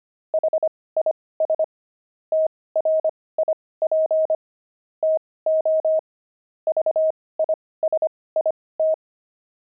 Morse code, heard as a keyed tone, about 25 wpm, spaced as for 18 wpm, 630 Hz. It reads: HIS TRIP TO VISIT
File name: pangram-fs-18-cs-25-kidding